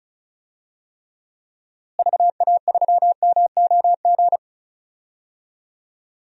Morse code, heard as a keyed tone, about 34 wpm, 700 Hz. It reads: VA3MOG
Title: DXCC-100 Call Signs - 6 Characters - Repeated 2x - 35wpm